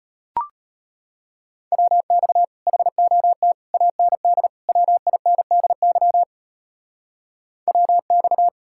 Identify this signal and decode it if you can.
Morse code, heard as a keyed tone, about 32 wpm, 700 Hz. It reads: WX HOT AND WINDY WX